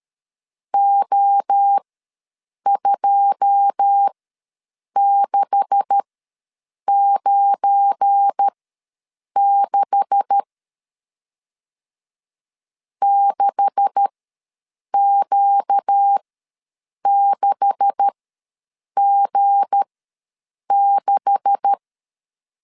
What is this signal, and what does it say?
Morse code, heard as a keyed tone, about 13 wpm, spaced as for 7 wpm, 800 Hz. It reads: O2696 6Q6G6
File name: LEZIONI / 9 286 ws 12wpm 3351